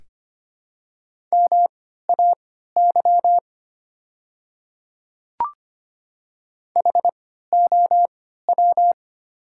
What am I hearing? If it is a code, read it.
Morse code, heard as a keyed tone, about 25 wpm, spaced as for 14 wpm, 700 Hz. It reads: MAY HOW
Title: 3x Character Spacing - Top 200 Words - 25wpm